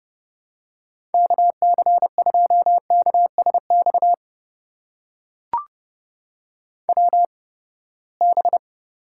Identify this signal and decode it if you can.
Morse code, heard as a keyed tone, about 30 wpm, spaced as for 21 wpm, 700 Hz. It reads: KC2KSX WB
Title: USA Call Signs - 6 Characters - 8x Character Spacing - 30wpm